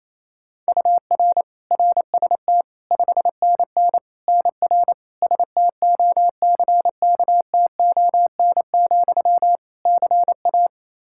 Morse code, heard as a keyed tone, about 27 wpm, 700 Hz. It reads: UR RST 5NN NR STOCKTON, CA
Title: Practice QSO - Exchange 1 - Beginning - 28wpm